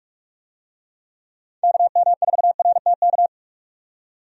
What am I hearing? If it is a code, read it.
Morse code, heard as a keyed tone, about 44 wpm, 700 Hz. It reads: KM4RTK
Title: USA Call Signs - 6 Characters - 45wpm